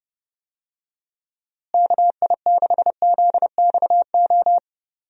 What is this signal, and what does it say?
Morse code, heard as a keyed tone, about 30 wpm, 700 Hz. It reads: KI6ZXO
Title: USA Call Signs - 6 Characters - Repeated 3x - 30wpm